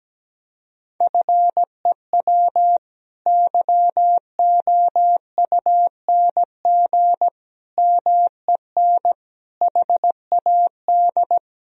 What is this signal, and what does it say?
Morse code, heard as a keyed tone, about 17 wpm, 700 Hz. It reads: FEW YOUNG MEN HAD
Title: Sentences from Top 1000 Words - Review Part 1 - 17wpm